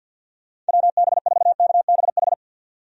Morse code, heard as a keyed tone, about 50 wpm, 700 Hz. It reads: WB4KBH
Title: USA Call Signs - 6 Characters - 50wpm